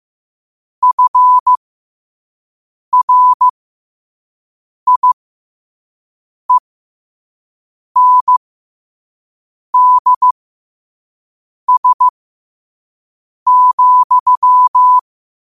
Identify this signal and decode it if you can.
Morse code, heard as a keyed tone, about 15 wpm, spaced as for 5 wpm, 1000 Hz. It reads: FRIENDS,
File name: qotdfiles / 5wpm